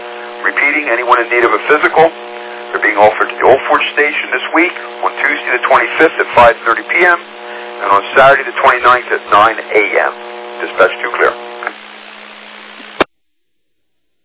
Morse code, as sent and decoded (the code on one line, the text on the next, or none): none